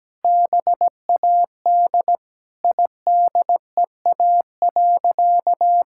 Morse code, heard as a keyed tone, about 17 wpm, 700 Hz. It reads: BAD IDEA.
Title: Sentences from Top 1000 Words - Review Part 5 - 17wpm